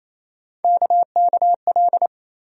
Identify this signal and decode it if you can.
Morse code, heard as a keyed tone, about 28 wpm, 700 Hz. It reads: KKL
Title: USA Call Sign Postfixes - 3 Letters 28wpm